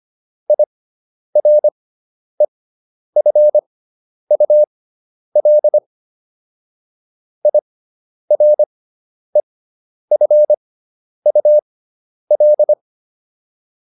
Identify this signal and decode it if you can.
Morse code, heard as a keyed tone, about 25 wpm, spaced as for 10 wpm, 600 Hz. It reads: IREFUL IREFUL
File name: IR6 10